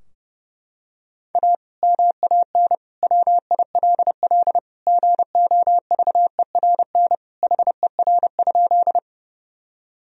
Morse code, heard as a keyed tone, about 30 wpm, 700 Hz. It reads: A MAN WILL GOVERN HER?
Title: Sentences from Top 1000 Words - Review Part 2 - 30wpm